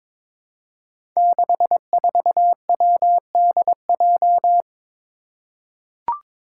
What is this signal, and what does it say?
Morse code, heard as a keyed tone, about 22 wpm, 700 Hz. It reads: 64WDJ
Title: Five Challenging Letter-Number Combinations 22wpm